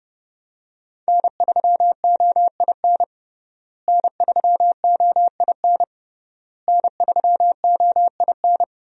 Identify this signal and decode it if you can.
Morse code, heard as a keyed tone, about 30 wpm, 700 Hz. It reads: N3OIN N3OIN N3OIN
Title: USA Call Signs - 5 Characters - Repeated 3x - 30wpm